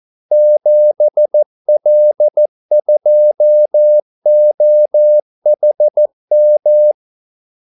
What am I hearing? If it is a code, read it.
Morse code, heard as a keyed tone, about 14 wpm, 600 Hz. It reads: 7L2OHM
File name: Callsigns-14wpm